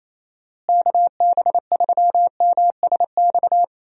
Morse code, heard as a keyed tone, about 28 wpm, 700 Hz. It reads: KB3MSX
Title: USA Call Signs - 6 Characters - Repeated 3x - 28wpm